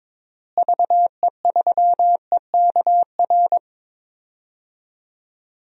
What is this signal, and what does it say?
Morse code, heard as a keyed tone, about 22 wpm, 700 Hz. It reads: VE3EKR
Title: DXCC-100 Call Signs - 6 Characters - Repeated 3x - 22wpm